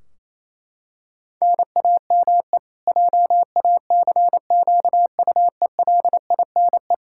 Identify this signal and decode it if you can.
Morse code, heard as a keyed tone, about 28 wpm, 700 Hz. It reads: NAME JACQUELINE
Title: Practice QSO - Exchange 1 - Beginning - 28wpm